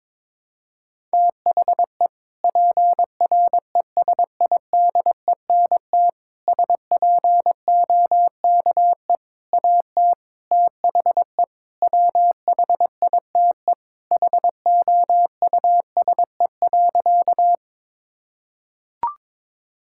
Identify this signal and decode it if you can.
Morse code, heard as a keyed tone, about 22 wpm, 700 Hz. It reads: THE PRESIDENT SPOKE AT THE WHITE HOUSE.
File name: practice / 22wpm / Sentences with 4-Word Phrases - 22wpm